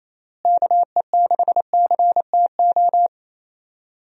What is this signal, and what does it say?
Morse code, heard as a keyed tone, about 28 wpm, 700 Hz. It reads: KE6CTO
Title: USA Call Signs - 6 Characters - Repeated 2x - 28wpm